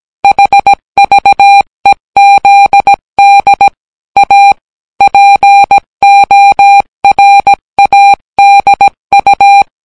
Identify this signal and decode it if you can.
Morse code, heard as a keyed tone, about 17 wpm, 800 Hz. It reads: HVEZD A PORADU